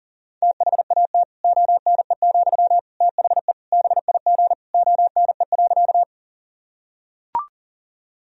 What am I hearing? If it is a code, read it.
Morse code, heard as a keyed tone, about 40 wpm, 700 Hz. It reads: THAT ONE, THE BIG ONE.
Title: Sentences from Top 1000 Words - Review Part 1 - 40wpm